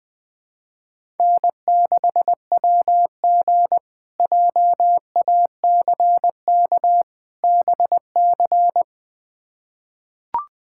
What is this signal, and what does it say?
Morse code, heard as a keyed tone, about 20 wpm, 700 Hz. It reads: N6WG JACK BC
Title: K1USN SST QSO Practice - 20wpm